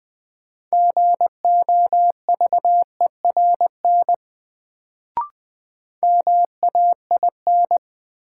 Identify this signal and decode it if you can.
Morse code, heard as a keyed tone, about 20 wpm, 700 Hz. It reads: GOVERN MAIN
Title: Sentences from Top 1000 Words - Review Part 2 - New Word Drill - 20wpm